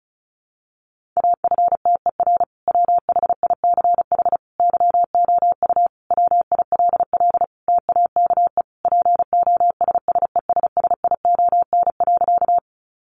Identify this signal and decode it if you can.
Morse code, heard as a keyed tone, about 34 wpm, 700 Hz. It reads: AFTER WHICH YOU WILL TAKE POSSESSION.
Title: Sentences from Top 1400 Words 35wpm